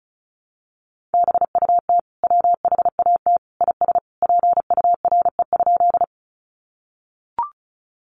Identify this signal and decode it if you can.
Morse code, heard as a keyed tone, about 34 wpm, 700 Hz. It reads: BUT WHAT IS PURE?
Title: Sentences from Top 1500 Words 35wpm